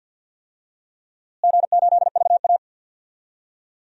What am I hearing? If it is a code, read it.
Morse code, heard as a keyed tone, about 49 wpm, 700 Hz. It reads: G8VA
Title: DXCC-100 Call Signs - 4 Characters - Repeated 3x - 50wpm